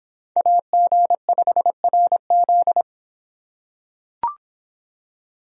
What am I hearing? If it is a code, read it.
Morse code, heard as a keyed tone, about 26 wpm, 700 Hz. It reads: AG5RZ